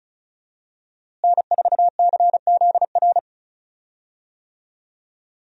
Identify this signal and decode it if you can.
Morse code, heard as a keyed tone, about 36 wpm, 700 Hz. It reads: N4CZR
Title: USA Call Signs - 5 Characters - Repeated 2x - 35wpm